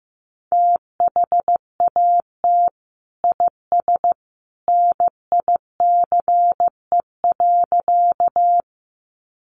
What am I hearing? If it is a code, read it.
Morse code, heard as a keyed tone, about 15 wpm, 700 Hz. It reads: THAT IS NICE.
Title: Sentences from Top 1100 Words 15wpm